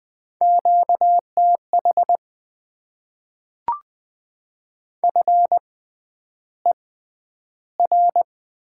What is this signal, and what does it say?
Morse code, heard as a keyed tone, about 20 wpm, spaced as for 10 wpm, 700 Hz. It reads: QTH FER
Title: Top 20 QSO Elements - 6x Character Spacing - 20wpm